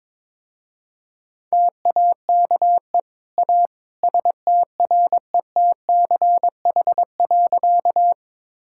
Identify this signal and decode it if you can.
Morse code, heard as a keyed tone, about 22 wpm, 700 Hz. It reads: TAKE A STRETCH.
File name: practice / 22wpm / Easy Sentences with Hints - 22wpm